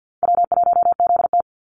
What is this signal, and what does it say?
Morse code, heard as a keyed tone, about 51 wpm, 700 Hz. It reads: W1ZT
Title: Easy Call Signs - 50wpm